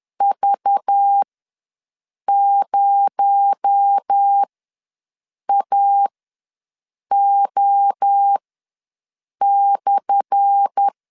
Morse code, heard as a keyed tone, about 10 wpm, spaced as for 6 wpm, 800 Hz. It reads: V0AO/
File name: LEZIONI / 10 SIGNS / 10D COMA ws 10wpm 4314